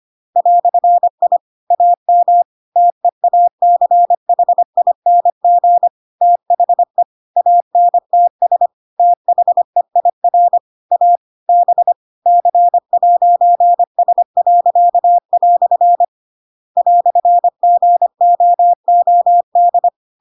Morse code, heard as a keyed tone, about 25 wpm, 700 Hz. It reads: "I AM TEACHING THE ANTS THEIR A B C'S." "GOOD
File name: Pinocchio / Pinocchio ch2 25 25 ik1hge